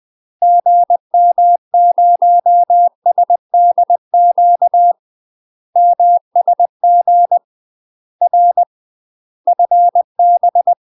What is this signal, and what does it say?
Morse code, heard as a keyed tone, about 20 wpm, 700 Hz. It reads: GM0SDQ MSG R FB
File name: UK MASTER QSOs TRAINING / 20WPM / UK MASTER QSOs 0023